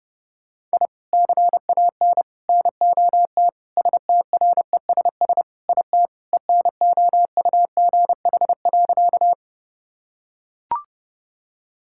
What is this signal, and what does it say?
Morse code, heard as a keyed tone, about 32 wpm, 700 Hz. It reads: I CAN NOT STRESS IT ENOUGH.
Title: Sentences from Top 1000 Words - Review Part 6 - 30wpm